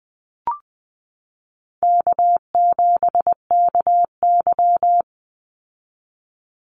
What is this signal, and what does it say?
Morse code, heard as a keyed tone, about 20 wpm, 700 Hz. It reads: K7KY
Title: Call Signs Repeated 3x - 20wpm